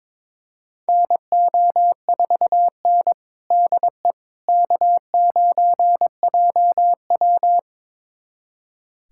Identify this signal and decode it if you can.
Morse code, heard as a keyed tone, about 22 wpm, 700 Hz. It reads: NO4N DE K9JW
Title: USA Callsign DE Callsign - 4 Characters - 22wpm